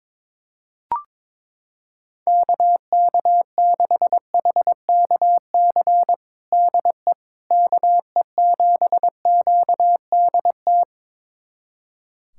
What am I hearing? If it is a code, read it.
Morse code, heard as a keyed tone, about 22 wpm, 700 Hz. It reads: KK6HKC DE KE7QDT
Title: USA Callsign DE Callsign - 6 Characters - 22wpm